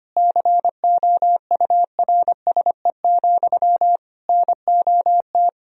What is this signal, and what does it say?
Morse code, heard as a keyed tone, about 25 wpm, 700 Hz. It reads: COURSE, NOT
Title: Sentences from Top 1000 Words - Review Part 4 - 25wpm